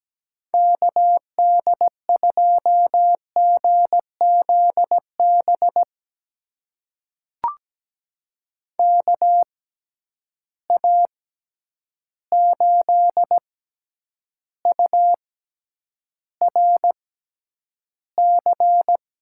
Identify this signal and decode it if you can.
Morse code, heard as a keyed tone, about 17 wpm, spaced as for 9 wpm, 700 Hz. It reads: KD2GZB KA8URC
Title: USA Call Signs - 6 Characters - 6x Character Spacing - 17wpm